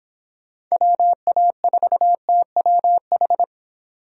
Morse code, heard as a keyed tone, about 26 wpm, 700 Hz. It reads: WA4TWH